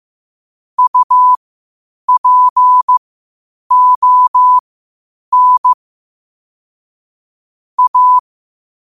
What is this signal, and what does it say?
Morse code, heard as a keyed tone, about 15 wpm, spaced as for 8 wpm, 1000 Hz. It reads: UPON A
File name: qotdfiles / 8wpm